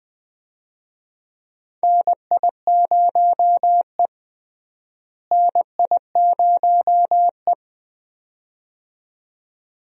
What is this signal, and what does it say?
Morse code, heard as a keyed tone, about 20 wpm, 700 Hz. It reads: NI0E NI0E
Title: USA Call Signs - 4 Characters - Repeated 2x - 20wpm